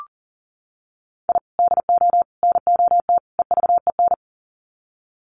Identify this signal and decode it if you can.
Morse code, heard as a keyed tone, about 40 wpm, 700 Hz. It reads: I DO NOT EVEN
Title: Sets of 4 Words 40wpm